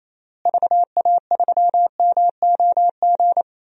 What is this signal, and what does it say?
Morse code, heard as a keyed tone, about 28 wpm, 700 Hz. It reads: VA3MOG
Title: DXCC-100 Call Signs - 6 Characters - Repeated 2x - 28wpm